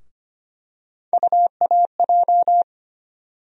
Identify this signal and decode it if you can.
Morse code, heard as a keyed tone, about 25 wpm, 700 Hz. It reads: UAJ
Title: USA Call Sign Postfixes - 3 Letters 25wpm